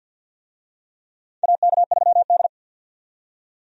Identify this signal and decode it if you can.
Morse code, heard as a keyed tone, about 48 wpm, 700 Hz. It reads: AK3D
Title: USA Call Signs - 4 Characters - 50wpm